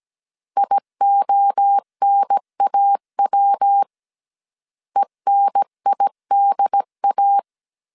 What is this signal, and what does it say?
Morse code, heard as a keyed tone, about 17 wpm, 800 Hz. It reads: IONAW ENIDA